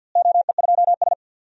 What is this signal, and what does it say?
Morse code, heard as a keyed tone, about 52 wpm, 700 Hz. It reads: OE2S
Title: DXCC-100 Call Signs - 4 Characters - Repeated 3x - 50wpm